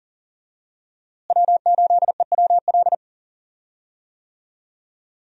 Morse code, heard as a keyed tone, about 40 wpm, 700 Hz. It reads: W8EWL
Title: USA Call Signs - 5 Characters - Repeated 3x - 40wpm